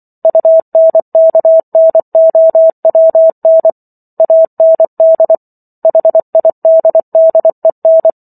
Morse code, heard as a keyed tone, about 24 wpm, 650 Hz. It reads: UNKNOWN AND HIDDEN
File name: Chapter 15 - 24 WPM